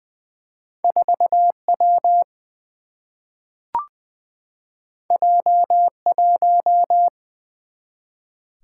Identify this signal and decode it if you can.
Morse code, heard as a keyed tone, about 20 wpm, 700 Hz. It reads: 4W J1
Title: Single-Letter Single-Number 20wpm